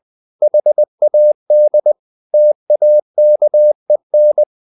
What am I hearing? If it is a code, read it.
Morse code, heard as a keyed tone, about 20 wpm, 600 Hz. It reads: HAD TAKEN
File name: the old man and the sea / the old man and the sea part 1 20wpm